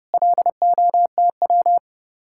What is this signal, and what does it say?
Morse code, heard as a keyed tone, about 38 wpm, 700 Hz. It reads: LOTW